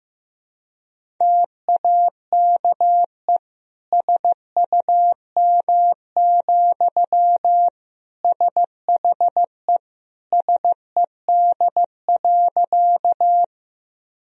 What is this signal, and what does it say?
Morse code, heard as a keyed tone, about 16 wpm, 700 Hz. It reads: TAKE SUM, SHE SED.
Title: Sentences from Top 200 Words and Abbreviations 15wpm